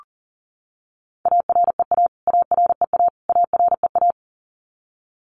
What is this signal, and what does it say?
Morse code, heard as a keyed tone, about 40 wpm, 700 Hz. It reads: AREA AREA AREA
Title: Top 500 Words Repeated 40wpm